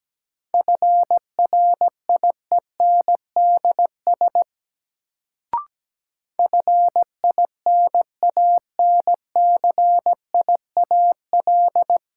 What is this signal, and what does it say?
Morse code, heard as a keyed tone, about 17 wpm, 700 Hz. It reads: FRIENDS FINANCIAL
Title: Sentences from Top 1000 Words - Review Part 5 - New Word Drill - 17wpm